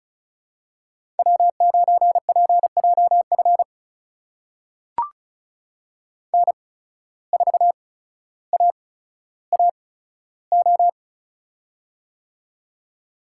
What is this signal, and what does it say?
Morse code, heard as a keyed tone, about 36 wpm, spaced as for 15 wpm, 700 Hz. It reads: W9PJF N4AAO
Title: USA Call Signs - 5 Characters - 8x Character Spacing - 35wpm